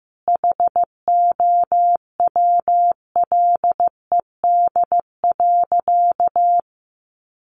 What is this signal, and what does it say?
Morse code, heard as a keyed tone, about 15 wpm, 700 Hz. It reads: HOWLED.
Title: A Sound Of Thunder 15wpm